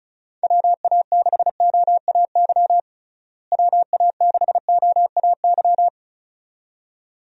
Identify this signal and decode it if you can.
Morse code, heard as a keyed tone, about 34 wpm, 700 Hz. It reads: WA6OAY WA6OAY
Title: USA Call Signs - 6 Characters - Repeated 2x - 35wpm